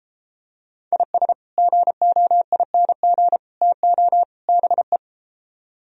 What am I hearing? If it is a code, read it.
Morse code, heard as a keyed tone, about 34 wpm, 700 Hz. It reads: IS GOING TO BE